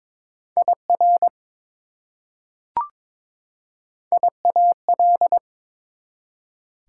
Morse code, heard as a keyed tone, about 22 wpm, 700 Hz. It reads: IR IAL
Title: Prefixes and Suffixes 22wpm